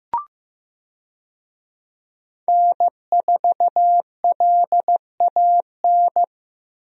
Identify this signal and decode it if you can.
Morse code, heard as a keyed tone, about 15 wpm, 700 Hz. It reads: N4LAN
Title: USA Call Signs - 5 Characters - Repeated 2x - 15wpm